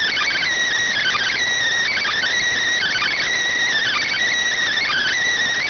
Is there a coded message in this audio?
Yes: Emergency Alert System (SAME) tones (header bursts) from 0.9 to 5.2 s, heard as harsh buzzing bursts.